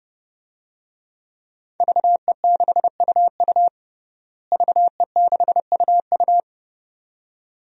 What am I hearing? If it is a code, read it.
Morse code, heard as a keyed tone, about 30 wpm, 700 Hz. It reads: VE6UU VE6UU
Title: DXCC-100 Call Signs - 5 Characters - Repeated 2x - 30wpm